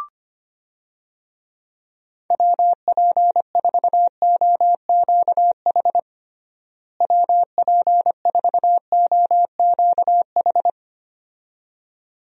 Morse code, heard as a keyed tone, about 25 wpm, 700 Hz. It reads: WP4OQH WP4OQH
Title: USA Call Signs - 6 Characters - Repeated 2x - 25wpm